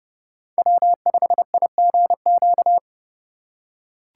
Morse code, heard as a keyed tone, about 32 wpm, 700 Hz. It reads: W5IGQ
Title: USA Call Signs - 5 Characters - 4x Character Spacing - 30wpm